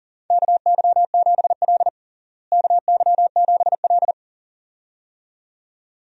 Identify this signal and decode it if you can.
Morse code, heard as a keyed tone, about 40 wpm, 700 Hz. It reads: KY7L KY7L
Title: USA Call Signs - 4 Characters - Repeated 2x - 40wpm